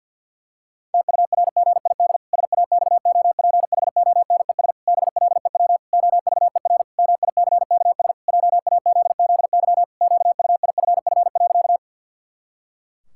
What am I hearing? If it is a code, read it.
Morse code, heard as a keyed tone, about 50 wpm, 700 Hz. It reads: TURGID SAXOPHONES BLEW OVER MICKS JAZZY QUIFF.
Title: Pangram Challenge v2 - 50wpm